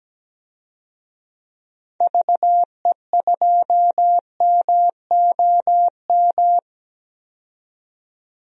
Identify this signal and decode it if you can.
Morse code, heard as a keyed tone, about 17 wpm, 700 Hz. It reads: VE2MOM
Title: DXCC-100 Call Signs - 6 Characters - 17wpm